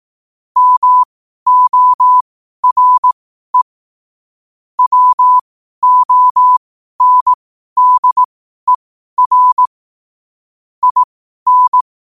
Morse code, heard as a keyed tone, about 18 wpm, spaced as for 12 wpm, 1000 Hz. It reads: MORE WONDER IN
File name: qotdfiles / 12wpm